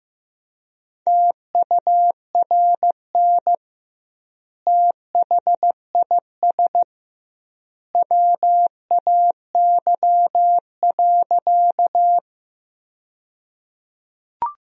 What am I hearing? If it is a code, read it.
Morse code, heard as a keyed tone, about 15 wpm, 700 Hz. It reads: TURN THIS WAY.
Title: 2x Word-Spacing - Sentences from Top 200 Words - 15wpm